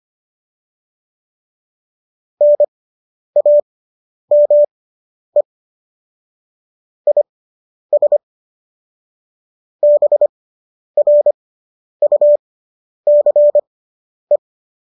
Morse code, heard as a keyed tone, about 25 wpm, spaced as for 10 wpm, 590 Hz. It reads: NAME IS BRUCE